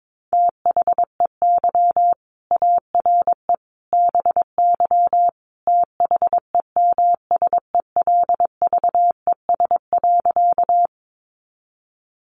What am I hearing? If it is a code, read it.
Morse code, heard as a keyed tone, about 22 wpm, 700 Hz. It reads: THEY ARE BY THEMSELVES.
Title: Sentences from Top 800 Words 22wpm